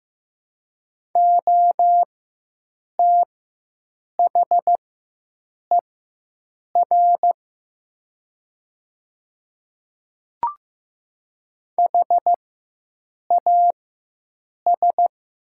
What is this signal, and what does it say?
Morse code, heard as a keyed tone, about 15 wpm, spaced as for 7 wpm, 700 Hz. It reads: OTHER HAS
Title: 4x Character Spacing - Top 100 Words - 15wpm